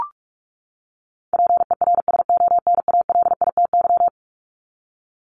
Morse code, heard as a keyed tone, about 44 wpm, 700 Hz. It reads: PERSONALITY
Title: Top 1251-1500 Words 45wpm